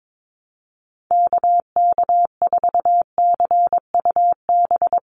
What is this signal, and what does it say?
Morse code, heard as a keyed tone, about 22 wpm, 700 Hz. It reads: KK4CUB
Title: USA Call Signs 22wpm